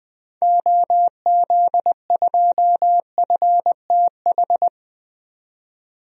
Morse code, heard as a keyed tone, about 20 wpm, 700 Hz. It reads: OZ2FTH
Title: DXCC-100 Call Signs - 6 Characters - 20wpm